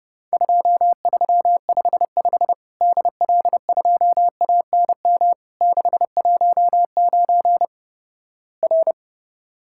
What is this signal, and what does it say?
Morse code, heard as a keyed tone, about 30 wpm, 700 Hz. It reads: 2355 DL2ANM 619 R
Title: Worked All Europe DX Contest - QTC Exchange - 30wpm